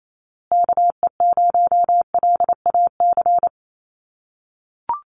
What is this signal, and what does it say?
Morse code, heard as a keyed tone, about 28 wpm, 700 Hz. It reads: KE0LAC
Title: USA Call Signs Repeated 3x 28wpm